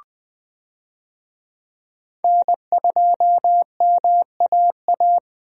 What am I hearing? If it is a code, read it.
Morse code, heard as a keyed tone, about 20 wpm, 700 Hz. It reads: N2MAA
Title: USA Call Signs - 5 Characters - 20wpm